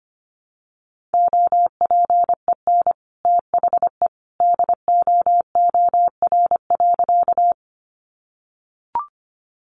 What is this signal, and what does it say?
Morse code, heard as a keyed tone, about 24 wpm, 700 Hz. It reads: OPEN THE DOOR.
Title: Sentences from Top 300 Words 25wpm